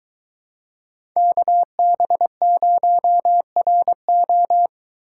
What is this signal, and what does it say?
Morse code, heard as a keyed tone, about 22 wpm, 700 Hz. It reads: KB0RO